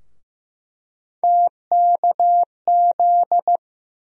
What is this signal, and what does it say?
Morse code, heard as a keyed tone, about 15 wpm, 700 Hz. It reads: TKZ
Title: USA Call Sign Postfixes - 3 Letters 15wpm